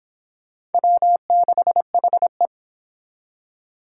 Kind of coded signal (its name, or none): Morse code